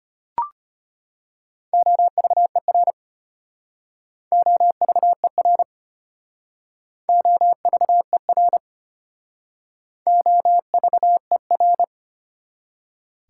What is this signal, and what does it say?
Morse code, heard as a keyed tone, about 32 wpm, 700 Hz. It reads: OVER OVER OVER OVER